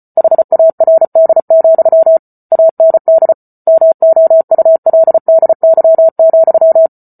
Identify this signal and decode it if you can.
Morse code, heard as a keyed tone, about 34 wpm, 650 Hz. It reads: HARD, AND MOULDY,